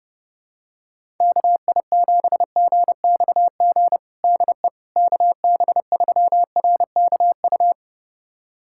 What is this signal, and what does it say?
Morse code, heard as a keyed tone, about 30 wpm, 700 Hz. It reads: KI7GXG DE KB3RKU
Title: USA Callsign DE Callsign 30wpm